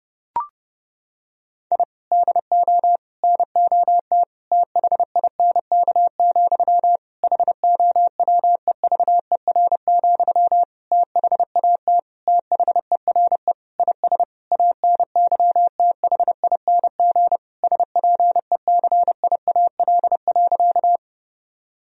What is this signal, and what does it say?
Morse code, heard as a keyed tone, about 31 wpm, 700 Hz. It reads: I DO NOT THINK, HOWEVER, THAT THERE IS ANYTHING SPECIAL.
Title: Sentences from Top 1000 Words - Review Part 3 - 30wpm